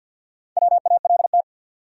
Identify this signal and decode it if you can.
Morse code, heard as a keyed tone, about 48 wpm, 700 Hz. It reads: WALT
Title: USA Common Names - Max 4 Letters - 50wpm